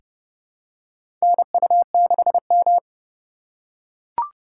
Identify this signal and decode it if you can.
Morse code, heard as a keyed tone, about 29 wpm, 700 Hz. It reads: NU6M
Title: USA Call Signs - 2x Character Spacing - 30wpm